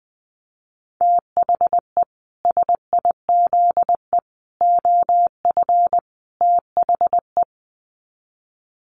Morse code, heard as a keyed tone, about 20 wpm, 700 Hz. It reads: THE SIZE OF THE
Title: Sets of 4 Words 20wpm